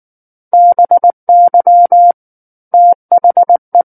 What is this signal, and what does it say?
Morse code, heard as a keyed tone, about 19 wpm, 700 Hz. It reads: BY THE
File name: Book One / 10-Chapter 10 - 19 WPM